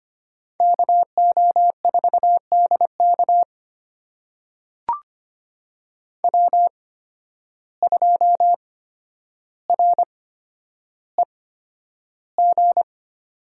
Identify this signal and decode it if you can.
Morse code, heard as a keyed tone, about 25 wpm, spaced as for 11 wpm, 700 Hz. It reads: KO4DK W2REG
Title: USA Call Signs - 5 Characters - 8x Character Spacing - 25wpm